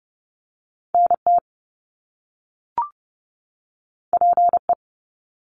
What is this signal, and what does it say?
Morse code, heard as a keyed tone, about 30 wpm, 700 Hz. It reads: NT PE
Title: Canadian Provinces and Territories 30wpm